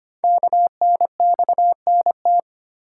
Morse code, heard as a keyed tone, about 25 wpm, 700 Hz. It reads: KNXNT
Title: Code Groups - 5 Letters - 25wpm